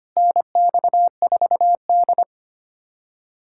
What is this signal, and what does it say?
Morse code, heard as a keyed tone, about 25 wpm, 700 Hz. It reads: NX4D
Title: USA Callsign DE Callsign - 4 Characters - 25wpm